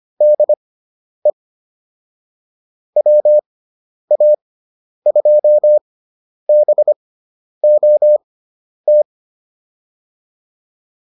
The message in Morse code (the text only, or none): DE WA2BOT